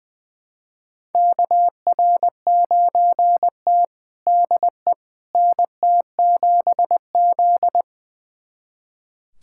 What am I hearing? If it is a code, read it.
Morse code, heard as a keyed tone, about 20 wpm, 700 Hz. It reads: KR9T DE NT7Z
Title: USA Callsign DE Callsign - 4 Characters - 20wpm